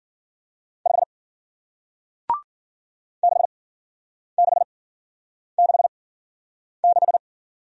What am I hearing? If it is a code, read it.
Morse code, heard as a keyed tone, about 53 wpm, spaced as for 9 wpm, 700 Hz. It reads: 5 6666